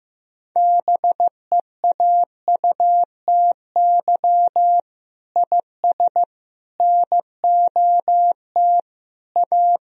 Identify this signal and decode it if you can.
Morse code, heard as a keyed tone, about 16 wpm, 700 Hz. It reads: BEAUTY IS NOT A
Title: Sentences from Top 1000 Words - Review Part 4 - 15wpm